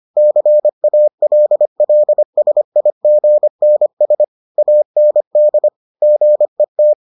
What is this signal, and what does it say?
Morse code, heard as a keyed tone, about 25 wpm, 600 Hz. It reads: CALLSIGNS AND GET